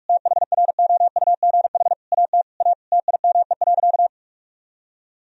Morse code, heard as a keyed tone, about 46 wpm, 700 Hz. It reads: THROUGH AT A TIME.
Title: Sentences from Top 1000 Words - Review Part 1 - 45wpm